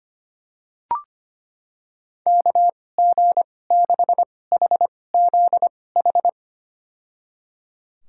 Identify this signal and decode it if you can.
Morse code, heard as a keyed tone, about 25 wpm, spaced as for 18 wpm, 700 Hz. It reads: KG6HZH
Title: USA Call Signs - 6 Characters - 2x Character Spacing - 25wpm